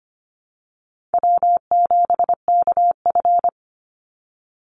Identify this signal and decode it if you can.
Morse code, heard as a keyed tone, about 25 wpm, 700 Hz. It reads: W7KF
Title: Call Signs - 25wpm